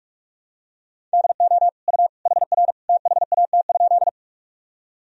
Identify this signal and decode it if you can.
Morse code, heard as a keyed tone, about 46 wpm, 700 Hz. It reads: DO U HR THAT?
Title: Sentences from Top 300 Words and Abbreviations 45wpm